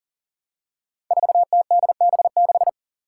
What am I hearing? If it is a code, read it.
Morse code, heard as a keyed tone, about 40 wpm, 700 Hz. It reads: 4TDB6
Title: Five Challenging Letter-Number Combinations 40wpm